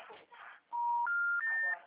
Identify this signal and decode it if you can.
Special information tone, with speech.
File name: out- -2023